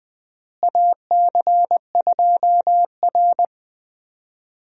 Morse code, heard as a keyed tone, about 20 wpm, 700 Hz. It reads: AC2R
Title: USA Call Signs - 4 Characters - Repeated 2x - 20wpm